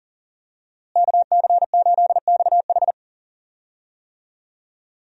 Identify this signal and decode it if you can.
Morse code, heard as a keyed tone, about 40 wpm, 700 Hz. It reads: KC8XH
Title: USA Call Signs - 5 Characters - Repeated 3x - 40wpm